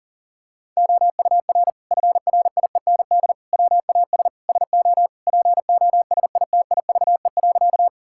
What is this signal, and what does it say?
Morse code, heard as a keyed tone, about 40 wpm, 700 Hz. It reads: OUR FRIEND WAS SO POSITIVE.
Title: Sentences from Top 1000 Words - Review Part 5 - 40wpm